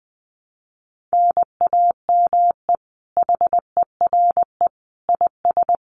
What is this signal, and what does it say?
Morse code, heard as a keyed tone, about 20 wpm, 700 Hz. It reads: NAME HERE IS
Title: Common QSO Phrases 20wpm